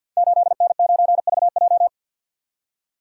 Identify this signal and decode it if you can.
Morse code, heard as a keyed tone, about 52 wpm, 690 Hz. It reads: 8N9VJ
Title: Code Groups - 5 Letter:Numbers - 50wpm